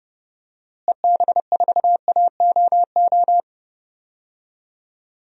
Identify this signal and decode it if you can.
Morse code, heard as a keyed tone, about 31 wpm, 700 Hz. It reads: EB4AOO
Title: DXCC-100 Call Signs - 6 Characters - Repeated 3x - 30wpm